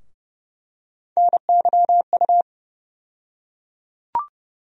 Morse code, heard as a keyed tone, about 30 wpm, 700 Hz. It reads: NYU
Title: USA Call Sign Postfixes - 3 Letters 30wpm